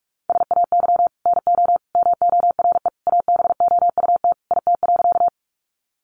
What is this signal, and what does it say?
Morse code, heard as a keyed tone, about 45 wpm, 700 Hz. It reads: SAY NO MORE ABOUT IT.
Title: Sentences from Top 200 Words 45wpm